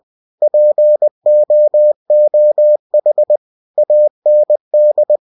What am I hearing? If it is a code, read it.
Morse code, heard as a keyed tone, about 20 wpm, 600 Hz. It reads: POOH AND
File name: cw-books / winnie-the-pooh / winnie-20wpm-page-0080